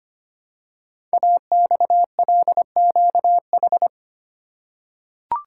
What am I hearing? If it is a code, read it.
Morse code, heard as a keyed tone, about 25 wpm, 700 Hz. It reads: AXLQH
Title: Code Groups - 5 Letters - 25wpm